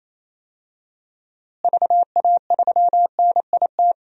Morse code, heard as a keyed tone, about 28 wpm, 700 Hz. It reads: VA3NIT
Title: DXCC-100 Call Signs - 6 Characters - Repeated 2x - 28wpm